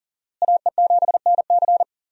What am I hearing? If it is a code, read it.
Morse code, heard as a keyed tone, about 40 wpm, 700 Hz. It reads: AE7NC
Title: USA Call Signs - 5 Characters - Repeated 2x - 40wpm